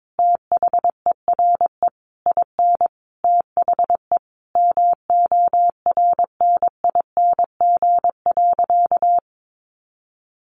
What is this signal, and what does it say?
Morse code, heard as a keyed tone, about 22 wpm, 700 Hz. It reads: THERE IN THE MORNING.
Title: Sentences from Top 1200 Words 22wpm